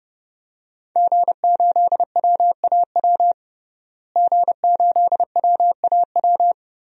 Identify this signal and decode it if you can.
Morse code, heard as a keyed tone, about 30 wpm, 700 Hz. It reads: G8WAW G8WAW
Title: DXCC-100 Call Signs - 5 Characters - Repeated 3x - 30wpm